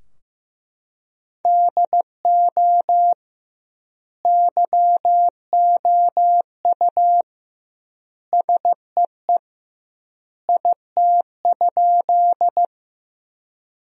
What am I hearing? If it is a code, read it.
Morse code, heard as a keyed tone, about 15 wpm, 700 Hz. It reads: DO YOU SEE IT?
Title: 2x Word-Spacing - Sentences from Top 100 Words - 15wpm